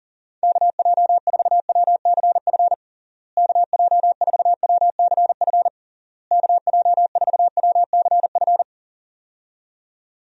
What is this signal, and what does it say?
Morse code, heard as a keyed tone, about 40 wpm, 700 Hz. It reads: KJ4WCF KJ4WCF KJ4WCF
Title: USA Call Signs - 6 Characters - Repeated 3x - 40wpm